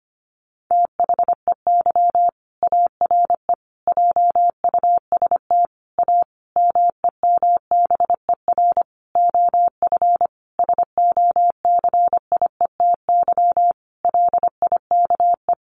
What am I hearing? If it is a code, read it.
Morse code, heard as a keyed tone, about 25 wpm, 700 Hz. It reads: THEY ARE JUST A MEMBER OF SOCIETY LIKE
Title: Sentences from Top 1100 Words 25wpm